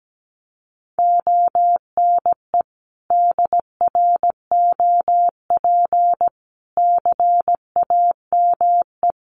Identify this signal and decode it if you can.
Morse code, heard as a keyed tone, about 17 wpm, 700 Hz. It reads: ONE DROP CAME
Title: Sentences from Top 1400 Words 17wpm